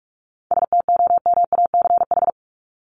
Morse code, heard as a keyed tone, about 44 wpm, 700 Hz. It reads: STOMACH
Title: Top 1501-1750 Words 45wpm